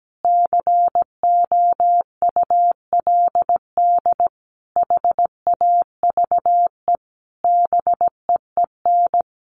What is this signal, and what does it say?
Morse code, heard as a keyed tone, about 17 wpm, 700 Hz. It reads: COULD HAVE BEEN
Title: Sentences from Top 1300 Words 17wpm